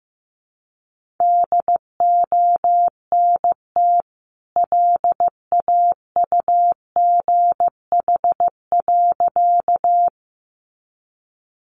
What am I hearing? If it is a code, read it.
Morse code, heard as a keyed tone, about 15 wpm, 700 Hz. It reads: DONT LAUGH.
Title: Sentences from Top 500 Words 15wpm